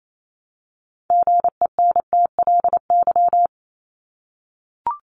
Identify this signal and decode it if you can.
Morse code, heard as a keyed tone, about 28 wpm, 700 Hz. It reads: GENTLY
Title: Top 1251-1500 Words 28wpm